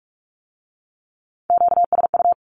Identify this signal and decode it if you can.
Morse code, heard as a keyed tone, about 46 wpm, 700 Hz. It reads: QSU
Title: Q-Codes with Spoken Meaning - 45wpm